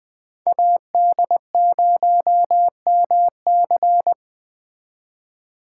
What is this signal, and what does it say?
Morse code, heard as a keyed tone, about 24 wpm, 700 Hz. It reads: AD0MC
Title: Speed Racing - USA 2x2 Call Signs 20wpm